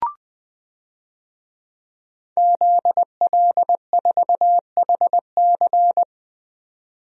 Morse code, heard as a keyed tone, about 21 wpm, 700 Hz. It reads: ZL4HC
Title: DXCC-100 Call Signs - 5 Characters - 20wpm